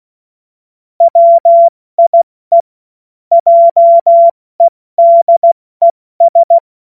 Morse code, heard as a keyed tone, about 16 wpm, 680 Hz. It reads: WIE JEDES
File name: Morsekurs / Witze / Witz 24. Januar 2023 70 F80